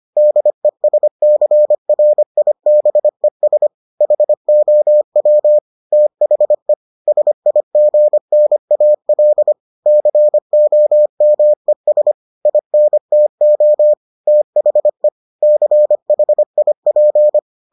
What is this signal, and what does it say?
Morse code, heard as a keyed tone, about 25 wpm, 600 Hz. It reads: DESCRIBES HOW THE SIGNAL COMES INTO THE CHIP